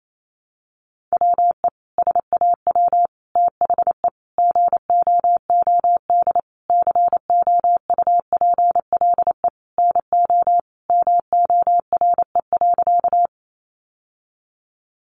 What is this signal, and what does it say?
Morse code, heard as a keyed tone, about 28 wpm, 700 Hz. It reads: WE SAW THE GOOD COUPLE NO MORE.
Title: Sentences from Top 1100 Words 28wpm